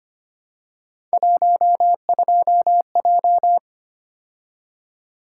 Morse code, heard as a keyed tone, about 25 wpm, 700 Hz. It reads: 12J
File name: Three Letter-Number Combinations - 25wpm